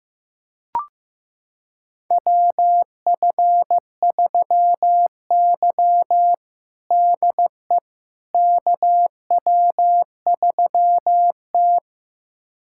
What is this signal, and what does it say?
Morse code, heard as a keyed tone, about 16 wpm, 700 Hz. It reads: WF3Y DE KW3T
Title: USA Callsign DE Callsign - 4 Characters - 15wpm